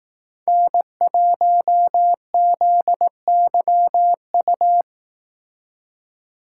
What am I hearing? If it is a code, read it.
Morse code, heard as a keyed tone, about 18 wpm, 700 Hz. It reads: N1ZYU
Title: Speed Racing - USA 1x3 Call Signs 15wpm